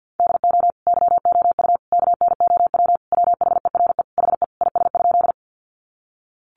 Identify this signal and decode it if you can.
Morse code, heard as a keyed tone, about 50 wpm, 700 Hz. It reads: DO YOU KNOW WHERE HE IS?
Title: Sentences from Top 300 Words 50wpm